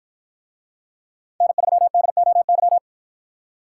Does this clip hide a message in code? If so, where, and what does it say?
Yes: Morse code from 1.4 to 2.8 s, heard as a keyed tone, about 53 wpm, 700 Hz. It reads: N3DOY